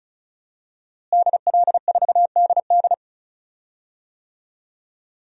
Morse code, heard as a keyed tone, about 36 wpm, 700 Hz. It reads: DL4DD
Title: DXCC-100 Call Signs - 5 Characters - Repeated 2x - 35wpm